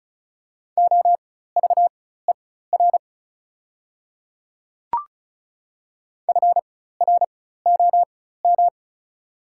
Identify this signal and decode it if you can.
Morse code, heard as a keyed tone, about 36 wpm, spaced as for 16 wpm, 700 Hz. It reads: OVER FROM
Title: 4x Character Spacing - Top 100 Words - 35wpm